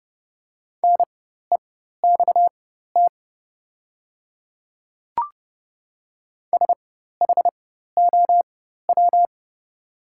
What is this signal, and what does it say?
Morse code, heard as a keyed tone, about 30 wpm, spaced as for 14 wpm, 690 Hz. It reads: NEXT SHOW
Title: 4x Character Spacing - Top 300 Words - 30wpm